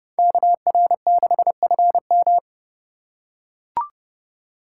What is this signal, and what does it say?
Morse code, heard as a keyed tone, about 30 wpm, 700 Hz. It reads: KR6FM
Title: USA Call Signs - 8x Character Spacing - 30wpm